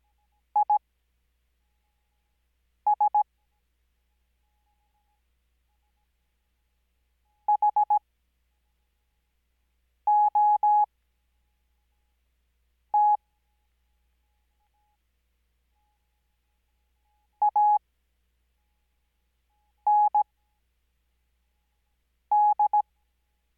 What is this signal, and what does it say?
Morse code, heard as a keyed tone, about 17 wpm, spaced as for 4 wpm, 840 Hz. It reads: IS HOT AND
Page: Farnsworth Spacing: 15WPM Character speed Spaced at 5WPM.